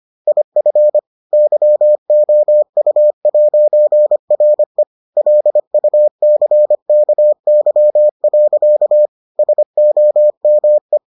Morse code, heard as a keyed tone, about 25 wpm, 600 Hz. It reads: IF YOU'RE LUCKY. SOME